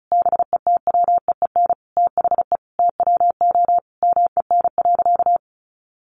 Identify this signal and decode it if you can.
Morse code, heard as a keyed tone, about 34 wpm, 700 Hz. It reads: BETWEEN THE TWO MEN.
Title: Sentences from Top 1300 Words 35wpm